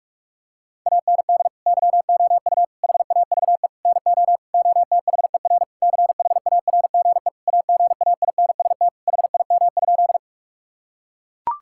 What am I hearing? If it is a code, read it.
Morse code, heard as a keyed tone, about 41 wpm, 700 Hz. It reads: AND YOU HAVE NO OTHER CHARGE AGAINST HIM?
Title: Sentences from Top 1000 Words - Review Part 6 - 45wpm